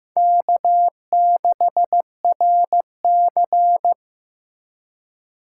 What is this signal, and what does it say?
Morse code, heard as a keyed tone, about 15 wpm, 700 Hz. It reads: K6RC
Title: K1USN SST QSO Practice - 15wpm